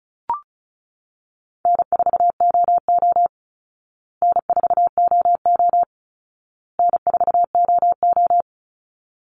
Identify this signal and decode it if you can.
Morse code, heard as a keyed tone, about 34 wpm, 700 Hz. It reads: N4OO N4OO N4OO
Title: Call Signs Repeated 2x - 35wpm